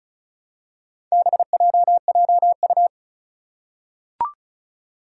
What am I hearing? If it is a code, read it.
Morse code, heard as a keyed tone, about 36 wpm, 700 Hz. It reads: BJJU
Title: Four Challenging Letter-Number Combinations 35wpm